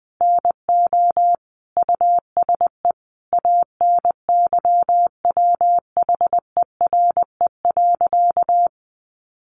Morse code, heard as a keyed tone, about 20 wpm, 700 Hz. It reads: NO USE ANYWHERE.
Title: Sentences from Top 1400 Words 20wpm